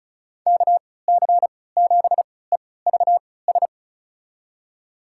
Morse code, heard as a keyed tone, about 36 wpm, spaced as for 20 wpm, 700 Hz. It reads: KC7EVS
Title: USA Call Signs - 6 Characters - 3x Character Spacing - 35wpm